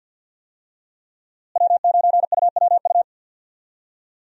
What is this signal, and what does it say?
Morse code, heard as a keyed tone, about 52 wpm, 700 Hz. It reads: W9UWU
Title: USA Call Signs - 5 Characters - Repeated 2x - 50wpm